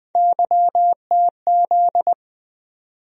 Morse code, heard as a keyed tone, about 20 wpm, 700 Hz. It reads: YTZ
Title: USA Call Sign Postfixes - 3 Letters 20wpm